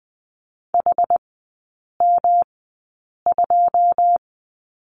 Morse code, heard as a keyed tone, about 20 wpm, spaced as for 8 wpm, 700 Hz. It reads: HM2
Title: 107 Lesson 18e - Letter m - Three Characters 20wpm